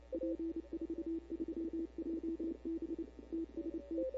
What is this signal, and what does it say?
Morse code, heard as a keyed tone, about 28 wpm, 340 Hz. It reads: P43JB TST
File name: CW / 2005-k5tr arrl cw dx -15:00